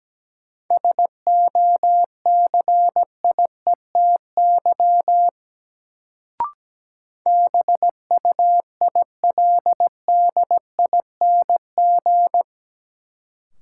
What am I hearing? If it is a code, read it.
Morse code, heard as a keyed tone, about 17 wpm, 700 Hz. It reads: SOCIETY BUILDING
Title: Sentences from Top 700 Words - New Word Drill - 17wpm